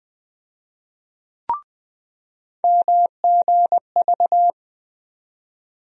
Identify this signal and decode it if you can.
Morse code, heard as a keyed tone, about 20 wpm, 700 Hz. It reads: MGV
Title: USA Call Sign Postfixes - 3 Letters 20wpm